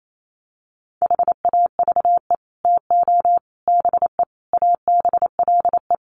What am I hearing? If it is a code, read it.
Morse code, heard as a keyed tone, about 28 wpm, 700 Hz. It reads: HAVE TO BE ABLE
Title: Sets of 5 Words 28wpm